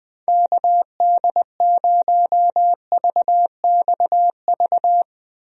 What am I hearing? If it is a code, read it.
Morse code, heard as a keyed tone, about 20 wpm, 700 Hz. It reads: KD0VXV
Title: USA Call Signs - 6 Characters - Repeated 3x - 20wpm